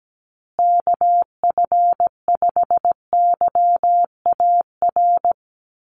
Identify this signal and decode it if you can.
Morse code, heard as a keyed tone, about 17 wpm, 700 Hz. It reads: KF5YAR
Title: USA Call Signs Repeated 3x 17wpm